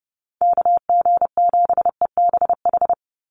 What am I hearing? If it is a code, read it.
Morse code, heard as a keyed tone, about 30 wpm, 700 Hz. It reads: KG7EBH
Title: USA Call Signs 30wpm